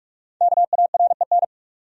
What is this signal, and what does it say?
Morse code, heard as a keyed tone, about 44 wpm, 700 Hz. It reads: KAREN
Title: USA Common Names - Max 5 Letters - 45wpm